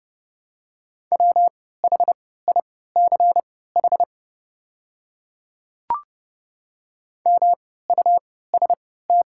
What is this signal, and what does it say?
Morse code, heard as a keyed tone, about 30 wpm, spaced as for 17 wpm, 700 Hz. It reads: WHICH MUST
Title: 3x Character Spacing - Top 300 Words - 30wpm